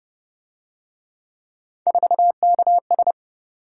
Morse code, heard as a keyed tone, about 30 wpm, 700 Hz. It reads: 4KS
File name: Speed-Racing - Three Letter-Number Combinations - 25wpm